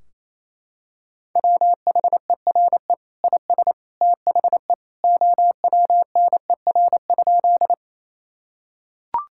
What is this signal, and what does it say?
Morse code, heard as a keyed tone, about 28 wpm, 700 Hz. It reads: WHERE IS THE OWNER?
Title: Sentences from Top 1600 Words 28wpm